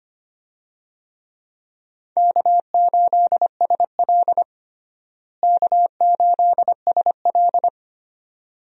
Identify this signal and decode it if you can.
Morse code, heard as a keyed tone, about 25 wpm, 700 Hz. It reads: K8SL K8SL
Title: USA Call Signs - 4 Characters - Repeated 2x - 25wpm